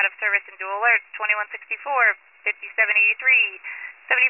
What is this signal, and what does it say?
Emergency Alert System (SAME) tones, heard as harsh buzzing bursts, with speech. It is heard from 1.2 to 4.0 s.